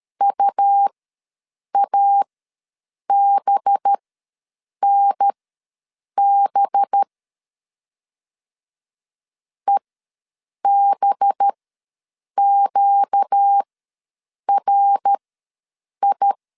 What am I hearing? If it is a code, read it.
Morse code, heard as a keyed tone, about 13 wpm, spaced as for 7 wpm, 800 Hz. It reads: UABNB EBQRI
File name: 5 LB ws 12wpm